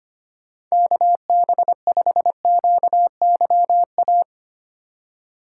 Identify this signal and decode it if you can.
Morse code, heard as a keyed tone, about 25 wpm, 700 Hz. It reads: KB5QYA
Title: USA Call Signs - 2x Character Spacing - 25wpm